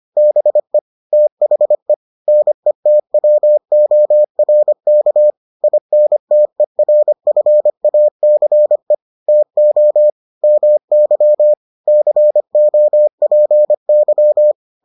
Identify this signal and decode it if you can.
Morse code, heard as a keyed tone, about 25 wpm, 600 Hz. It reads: BE THE NETWORK INTERFACE TO MY COPY